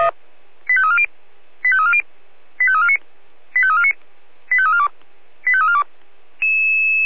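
Five-tone selective call (ZVEI), heard as quick runs of five tones, tones of about 70 ms. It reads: 85340 85340 85349 85349 85342 85342
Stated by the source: ZVEI five-tone selective call received on an SDR.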